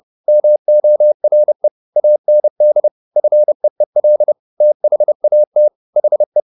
Morse code, heard as a keyed tone, about 30 wpm, 600 Hz. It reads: MORE AND FEEL THAT HE